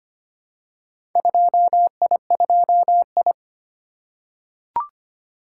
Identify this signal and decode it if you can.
Morse code, heard as a keyed tone, about 25 wpm, 700 Hz. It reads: 2I2I
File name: practice / 25wpm / Four Letter-Number Combinations - 25wpm